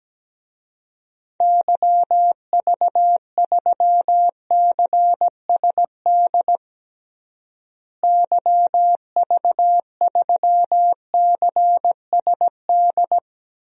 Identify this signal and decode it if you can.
Morse code, heard as a keyed tone, about 17 wpm, 700 Hz. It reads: YV3CSD YV3CSD
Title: DXCC-100 Call Signs - 6 Characters - Repeated 2x - 17wpm